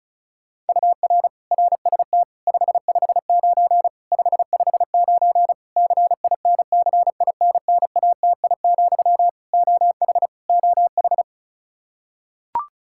Morse code, heard as a keyed tone, about 36 wpm, 700 Hz. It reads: UR RST 559 559 CINCINNATI, OH OH
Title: Practice QSO - Exchange 1 - Beginning - 35wpm